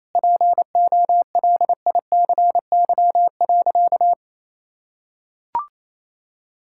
Morse code, heard as a keyed tone, about 27 wpm, 700 Hz. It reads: POLICY.
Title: Sentences from Top 1000 Words - Review Part 5 - 28wpm